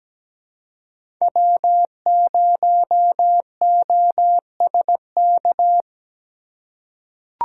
Morse code, heard as a keyed tone, about 17 wpm, 700 Hz. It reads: W0OSK
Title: USA Call Signs - 5 Characters - 3x Character Spacing - 17wpm